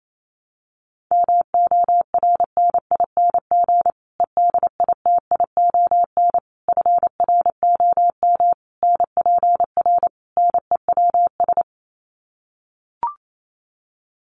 Morse code, heard as a keyed tone, about 28 wpm, 700 Hz. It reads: MORNING EDITION FROM NPR NEWS
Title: Sets of 5 Words Encore 28wpm